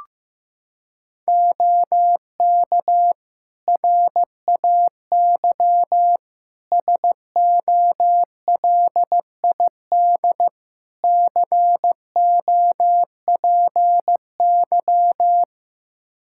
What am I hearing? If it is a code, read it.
Morse code, heard as a keyed tone, about 15 wpm, 700 Hz. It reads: OK RAY SOLID COPY
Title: Practice QSO - Exchange 3 - More Info - 15wpm